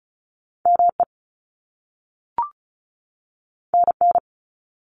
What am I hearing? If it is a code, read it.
Morse code, heard as a keyed tone, about 36 wpm, 700 Hz. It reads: ME NN
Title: Top Two Letter Combinations in Words 35wpm